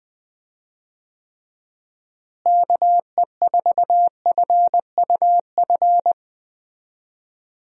Morse code, heard as a keyed tone, about 20 wpm, 700 Hz. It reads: KE4FUF
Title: USA Call Signs - 6 Characters - Repeated 2x - 20wpm